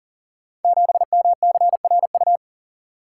Morse code, heard as a keyed tone, about 38 wpm, 690 Hz. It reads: 7MCRU
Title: Code Groups - 5 Letter:Numbers - 40wpm